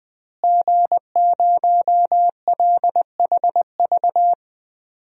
Morse code, heard as a keyed tone, about 20 wpm, 700 Hz. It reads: G0LHV
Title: DXCC-100 Call Signs - 5 Characters - Repeated 3x - 20wpm